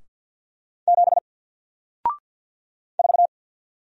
Morse code, heard as a keyed tone, about 50 wpm, 700 Hz. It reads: 7 4
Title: Single Letter-Number Alt 50wpm